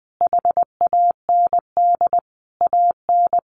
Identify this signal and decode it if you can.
Morse code, heard as a keyed tone, about 20 wpm, 700 Hz. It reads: HAND AN
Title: Sentences from Top 1400 Words 20wpm